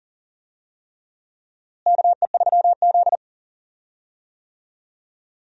Morse code, heard as a keyed tone, about 40 wpm, 700 Hz. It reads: KE3Z
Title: USA Call Signs - 4 Characters - Repeated 3x - 40wpm